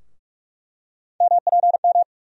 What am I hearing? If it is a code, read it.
Morse code, heard as a keyed tone, about 46 wpm, 700 Hz. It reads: MPM
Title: USA Call Sign Postfixes - 3 Letters 45wpm